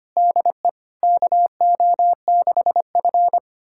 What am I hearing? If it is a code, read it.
Morse code, heard as a keyed tone, about 25 wpm, 700 Hz. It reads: DE KO6F
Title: USA Callsign DE Callsign - 4 Characters - 25wpm